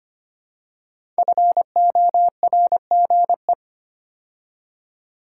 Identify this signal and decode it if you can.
Morse code, heard as a keyed tone, about 25 wpm, 700 Hz. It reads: FORGE
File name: 5 Letter Words - First 3 Letters Given - 25wpm